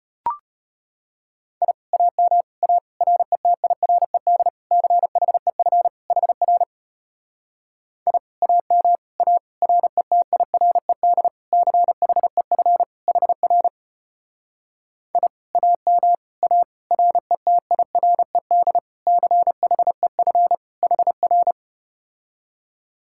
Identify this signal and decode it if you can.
Morse code, heard as a keyed tone, about 33 wpm, 700 Hz. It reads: I AM A RETIRED CHEF HR I AM A RETIRED CHEF HR I AM A RETIRED CHEF HR